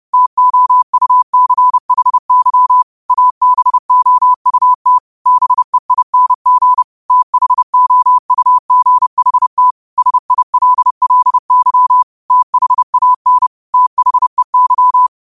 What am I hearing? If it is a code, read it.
Morse code, heard as a keyed tone, about 30 wpm, 1000 Hz. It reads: TOUCHY ABOUT BEING THOUGHT SILLY THAN THEY